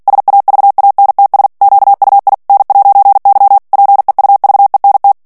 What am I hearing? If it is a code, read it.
Morse code, heard as a keyed tone, about 48 wpm, 800 Hz. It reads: SAVANTS QUI N'Y PEUVENT
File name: CW / FICHIERS-MP3 / FRANCAIS TEXTE CLAIR 2 VITESSE 45 MOTS MINUTES